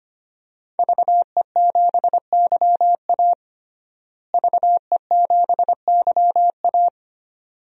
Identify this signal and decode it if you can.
Morse code, heard as a keyed tone, about 25 wpm, 700 Hz. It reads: VE7YA VE7YA
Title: DXCC-100 Call Signs - 5 Characters - Repeated 2x - 25wpm